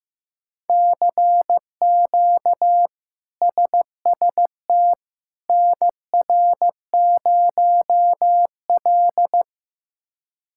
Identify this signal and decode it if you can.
Morse code, heard as a keyed tone, about 16 wpm, 700 Hz. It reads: CQ SST NR0L
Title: K1USN SST QSO Practice - 15wpm